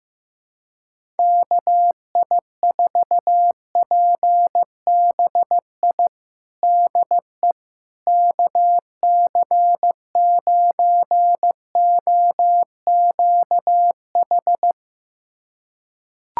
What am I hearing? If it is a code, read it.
Morse code, heard as a keyed tone, about 15 wpm, 700 Hz. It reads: KI4PBI DE KC9OQH
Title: USA Callsign DE Callsign 15wpm